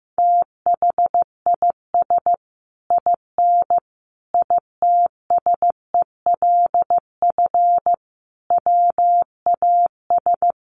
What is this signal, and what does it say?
Morse code, heard as a keyed tone, about 15 wpm, 700 Hz. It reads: THIS IN ITSELF WAS
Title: Sentences from Top 1400 Words 15wpm